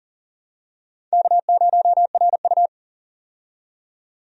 Morse code, heard as a keyed tone, about 40 wpm, 700 Hz. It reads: K0RU
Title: USA Call Signs - 4 Characters - 40wpm